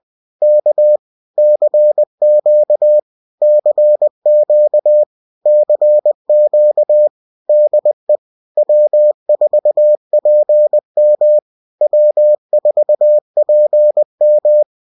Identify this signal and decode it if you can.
Morse code, heard as a keyed tone, about 20 wpm, 600 Hz. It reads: K CQ CQ CQ DE W4PM W4PM